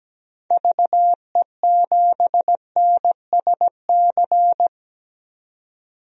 Morse code, heard as a keyed tone, about 17 wpm, 700 Hz. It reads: VE7NSC
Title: DXCC-100 Call Signs - 6 Characters - 17wpm